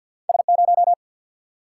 Morse code, heard as a keyed tone, about 50 wpm, 700 Hz. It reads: S0
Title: Single-Letter Single-Number 50wpm